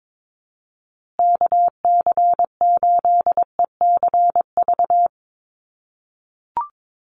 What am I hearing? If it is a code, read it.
Morse code, heard as a keyed tone, about 22 wpm, 700 Hz. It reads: KC8ECV
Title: USA Call Signs Repeated 2x 22wpm